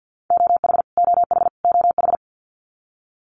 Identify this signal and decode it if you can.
Morse code, heard as a keyed tone, about 49 wpm, 700 Hz. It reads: OH OH OH
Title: US States Repeated 3x 50wpm